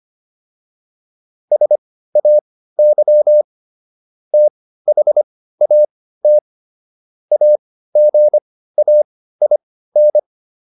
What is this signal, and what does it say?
Morse code, heard as a keyed tone, about 25 wpm, spaced as for 15 wpm, 600 Hz. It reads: SAY THAT AGAIN